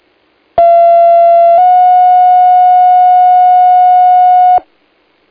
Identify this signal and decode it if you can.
Two-tone page, heard as a one-second tone, then a three-second tone, about 680 then 730 Hz. It is heard from 0.6 to 4.6 s.